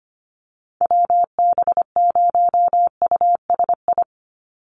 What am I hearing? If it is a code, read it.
Morse code, heard as a keyed tone, about 25 wpm, 700 Hz. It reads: WB0USI
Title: NA SOTA Chaser Callsigns 25wpm